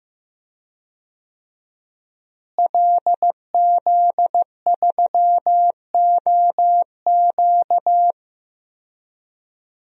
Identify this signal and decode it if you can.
Morse code, heard as a keyed tone, about 15 wpm, 700 Hz. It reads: LZ3OQ
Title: DXCC-100 Call Signs - 5 Characters - Repeated 3x - 15wpm